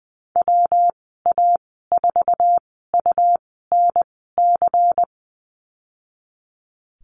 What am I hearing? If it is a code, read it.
Morse code, heard as a keyed tone, about 20 wpm, spaced as for 14 wpm, 700 Hz. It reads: WA4UNC
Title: 161 Lesson 26g - Letter g - Call Signs 20wpm